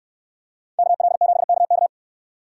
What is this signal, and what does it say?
Morse code, heard as a keyed tone, about 69 wpm, 700 Hz. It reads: KL7KK